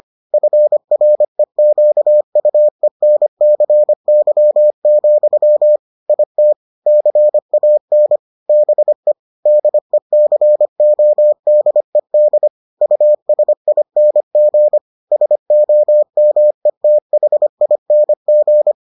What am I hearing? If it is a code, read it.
Morse code, heard as a keyed tone, about 25 wpm, 600 Hz. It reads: FREQUENCY, IT CAN BE DECODED USING SOMETHING